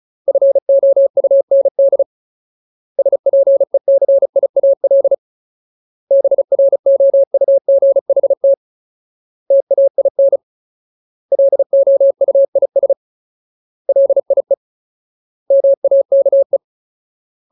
Morse code, heard as a keyed tone, about 34 wpm, 550 Hz. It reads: FOUND SPECIAL BROUGHT TAIN LOUIS LIE MAKE
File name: Ultimate Word List / 35WPM Ultimate